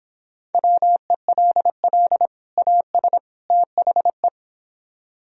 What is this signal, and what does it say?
Morse code, heard as a keyed tone, about 29 wpm, 700 Hz. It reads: WELL AS THE